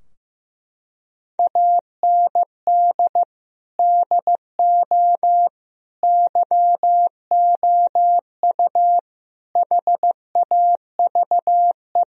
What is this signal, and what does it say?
Morse code, heard as a keyed tone, about 15 wpm, 700 Hz. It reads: AND DO YOU HAVE
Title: Sentences from Top 1000 Words - Review Part 3 - 15wpm